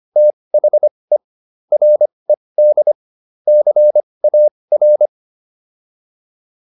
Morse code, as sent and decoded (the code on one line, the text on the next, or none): - .... / .-. -.. / -.-. .- .-.
THE RED CAR